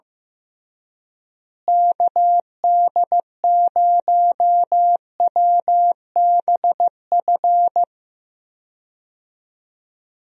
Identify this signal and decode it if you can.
Morse code, heard as a keyed tone, about 16 wpm, 700 Hz. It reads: KD0WBF
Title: USA Call Signs - 6 Characters - Repeated 3x - 15wpm